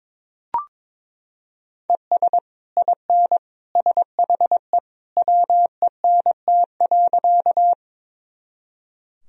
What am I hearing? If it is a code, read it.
Morse code, heard as a keyed tone, about 22 wpm, 700 Hz. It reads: ES IN SHE WENT.
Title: Sentences from Top 200 Words and Abbreviations 22wpm